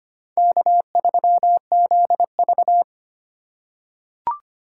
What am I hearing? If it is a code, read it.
Morse code, heard as a keyed tone, about 25 wpm, 700 Hz. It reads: K3ZV
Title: USA Call Signs - 4 Characters - 3x Character Spacing - 25wpm